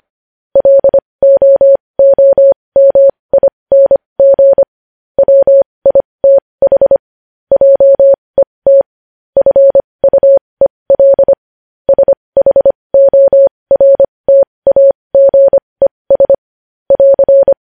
Morse code, heard as a keyed tone, about 25 wpm, spaced as for 20 wpm, 550 Hz. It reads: LOOMING WITH JET FUEL SHORTAGES <AR>
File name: podcast / News headlines-1x-25x20